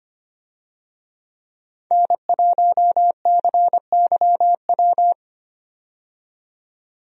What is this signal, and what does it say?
Morse code, heard as a keyed tone, about 25 wpm, 700 Hz. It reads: N1CYW
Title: USA Call Signs - 5 Characters - 25wpm